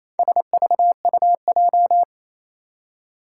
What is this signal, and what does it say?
Morse code, heard as a keyed tone, about 28 wpm, 700 Hz. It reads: SVUJ
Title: Four Challenging Letter-Number Combinations 28wpm